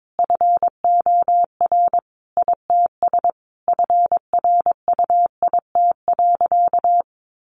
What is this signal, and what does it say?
Morse code, heard as a keyed tone, about 22 wpm, 700 Hz. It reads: FOR ITS FRUIT.
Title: Sentences from Top 1300 Words 22wpm